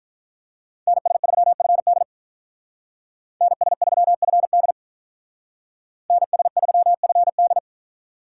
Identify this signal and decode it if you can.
Morse code, heard as a keyed tone, about 42 wpm, 700 Hz. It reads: NS3FD NS3FD NS3FD
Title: Speed Racing - USA 2x2 Call Signs 35wpm